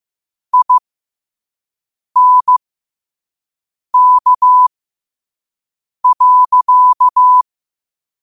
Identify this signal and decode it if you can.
Morse code, heard as a keyed tone, about 15 wpm, spaced as for 5 wpm, 1000 Hz. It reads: INK.
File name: qotdfiles / 5wpm